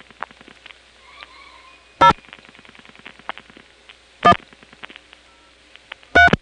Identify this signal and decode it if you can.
DTMF, heard as telephone touch tones, tones of about 90 ms. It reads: 0 1 3
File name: phonetrips / SmyrnaGA5HQ